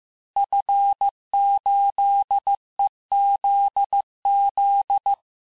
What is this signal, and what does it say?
Morse code, heard as a keyed tone, about 15 wpm, 800 Hz. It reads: F8EZZ
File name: cours-du-lundi files / cours-1585-texte-FIN-de-QSO-vitesse-14mpm